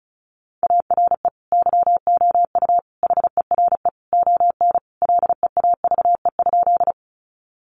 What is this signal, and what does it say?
Morse code, heard as a keyed tone, about 34 wpm, 700 Hz. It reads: ARE YOU HERE ON LEAVE?
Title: Sentences from Top 400 Words 35wpm